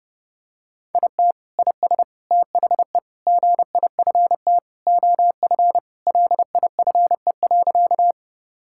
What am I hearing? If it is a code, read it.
Morse code, heard as a keyed tone, about 30 wpm, 700 Hz. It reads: IT IS THE GIFT OF LIFE.
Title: Sentences from Top 1000 Words - Review Part 6 - 30wpm